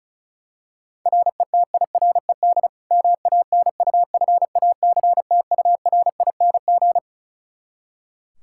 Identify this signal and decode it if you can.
Morse code, heard as a keyed tone, about 36 wpm, 700 Hz. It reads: RETIRED MANUFACTURING
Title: Practice QSO - Exchange 3 - More Info - 35wpm